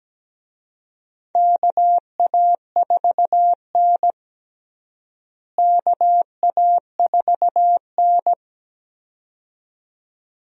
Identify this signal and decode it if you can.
Morse code, heard as a keyed tone, about 17 wpm, 700 Hz. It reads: KA4N KA4N
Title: USA Call Signs - 4 Characters - Repeated 2x - 17wpm